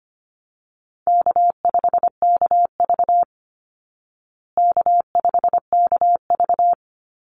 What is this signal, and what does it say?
Morse code, heard as a keyed tone, about 25 wpm, 700 Hz. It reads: K5KV K5KV
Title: Easy Call Signs Repeated 3x - 25wpm